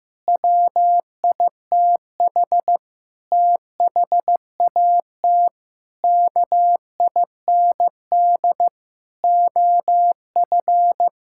Morse code, heard as a keyed tone, about 15 wpm, 700 Hz. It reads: WITH THAT KIND OF
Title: Sentences from Top 1000 Words - Review Part 5 - 15wpm